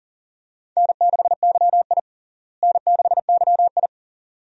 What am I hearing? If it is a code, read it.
Morse code, heard as a keyed tone, about 40 wpm, 700 Hz. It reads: N6YI N6YI
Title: USA Call Signs - 4 Characters - Repeated 2x - 40wpm